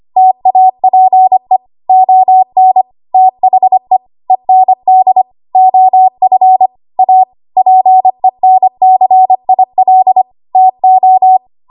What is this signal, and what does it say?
Morse code, heard as a keyed tone, about 25 wpm, 750 Hz. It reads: TAPE ON THE END OF A PENCIL TO
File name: Archive / 25 WPM / 25WPM